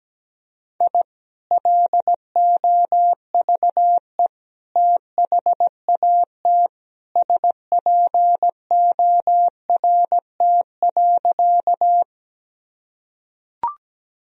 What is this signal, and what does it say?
Morse code, heard as a keyed tone, about 17 wpm, 700 Hz. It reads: I LOVE THAT SPORT.
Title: Sentences from Top 1000 Words - Review Part 4 - 17wpm